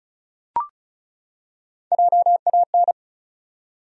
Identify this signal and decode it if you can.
Morse code, heard as a keyed tone, about 34 wpm, 700 Hz. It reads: JAN
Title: USA Common Names - Max 4 Letters - 35wpm